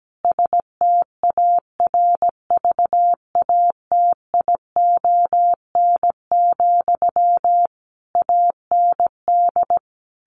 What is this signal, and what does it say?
Morse code, heard as a keyed tone, about 17 wpm, 700 Hz. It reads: STARVATION, AND
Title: A Sound Of Thunder 17wpm